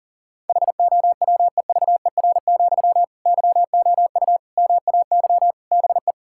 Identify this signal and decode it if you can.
Morse code, heard as a keyed tone, about 40 wpm, 700 Hz. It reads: HOWEVER, YOU MAY BE
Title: Sentences from Top 1600 Words 40wpm